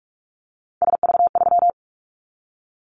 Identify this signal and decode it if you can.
Morse code, heard as a keyed tone, about 44 wpm, 700 Hz. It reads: SV3
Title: Callsign Prefixes for Top 50 DXCC - 45wpm